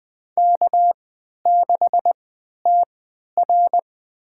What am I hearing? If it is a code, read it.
Morse code, heard as a keyed tone, about 21 wpm, spaced as for 12 wpm, 700 Hz. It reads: K6TR
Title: USA Call Signs - 4 Characters - 3x Character Spacing - 20wpm